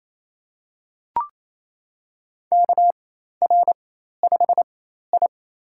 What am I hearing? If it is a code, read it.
Morse code, heard as a keyed tone, about 27 wpm, spaced as for 13 wpm, 700 Hz. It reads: KR5I
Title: USA Call Signs - 4x Character Spacing - 28wpm